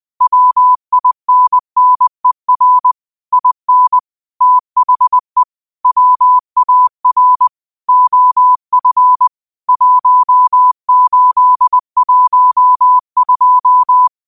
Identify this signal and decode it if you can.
Morse code, heard as a keyed tone, about 20 wpm, 1000 Hz. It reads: WINNER IN THE WAR OF 1812
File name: qotdfiles / 20wpm